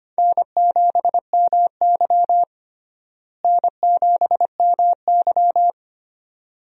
Morse code, heard as a keyed tone, about 25 wpm, 700 Hz. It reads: N7MY N7MY
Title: USA Call Signs - 4 Characters - Repeated 2x - 25wpm